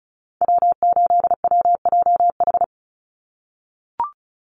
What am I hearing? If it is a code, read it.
Morse code, heard as a keyed tone, about 34 wpm, 700 Hz. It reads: W8WJH
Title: USA Call Signs 35wpm